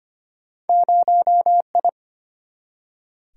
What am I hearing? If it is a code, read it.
Morse code, heard as a keyed tone, about 25 wpm, 700 Hz. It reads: 0I
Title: Single-Letter Single-Number 25wpm